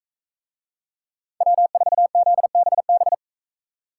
Morse code, heard as a keyed tone, about 44 wpm, 700 Hz. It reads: W4ZBB